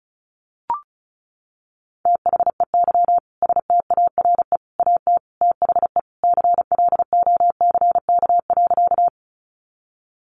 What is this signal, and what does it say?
Morse code, heard as a keyed tone, about 34 wpm, 700 Hz. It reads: THEY STARE AT THE CLOCK.
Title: Easy Phrases 35wpm